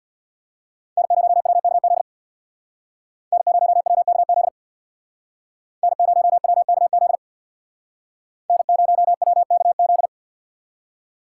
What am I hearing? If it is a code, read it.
Morse code, heard as a keyed tone, about 63 wpm, 700 Hz. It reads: N0WKZ N0WKZ N0WKZ N0WKZ